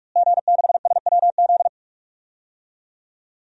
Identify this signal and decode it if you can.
Morse code, heard as a keyed tone, about 46 wpm, 700 Hz. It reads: G6SWZ
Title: DXCC-100 Call Signs - 5 Characters - Repeated 2x - 45wpm